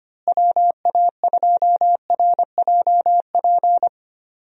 Morse code, heard as a keyed tone, about 25 wpm, 700 Hz. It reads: WA2RJP
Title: USA Call Signs - 6 Characters - Repeated 2x - 25wpm